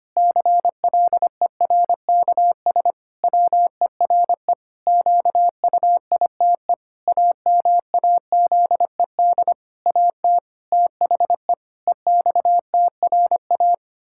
Morse code, heard as a keyed tone, about 25 wpm, 700 Hz. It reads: CLERKS WERE QUITE AMAZED AT THE EXTRA
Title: Pangram Challenge v2 - 25wpm